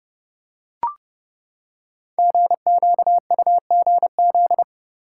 Morse code, heard as a keyed tone, about 30 wpm, 700 Hz. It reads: GQUGZ
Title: Code Groups - 5 Letters - 30wpm